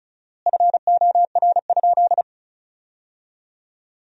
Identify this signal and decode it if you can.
Morse code, heard as a keyed tone, about 36 wpm, 700 Hz. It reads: FOR?
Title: 4x Word-Spacing - Sentences from Top 100 Words - 35wpm